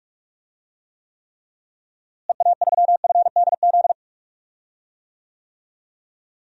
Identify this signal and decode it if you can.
Morse code, heard as a keyed tone, about 42 wpm, 700 Hz. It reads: EA3FDZ
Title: DXCC-100 Call Signs - 6 Characters - Repeated 3x - 45wpm